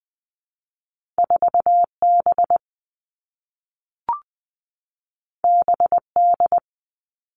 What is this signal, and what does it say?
Morse code, heard as a keyed tone, about 20 wpm, 700 Hz. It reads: 4 B B D
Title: Two Challenging Letter-Number Combinations 20wpm